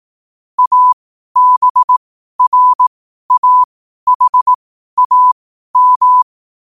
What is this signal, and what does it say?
Morse code, heard as a keyed tone, about 18 wpm, spaced as for 12 wpm, 1000 Hz. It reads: ABRAHAM
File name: qotdfiles / 12wpm